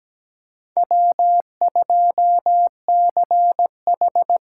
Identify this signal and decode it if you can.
Morse code, heard as a keyed tone, about 20 wpm, 700 Hz. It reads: W2CH
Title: Speed Racing - USA 1x2 Call Signs 17wpm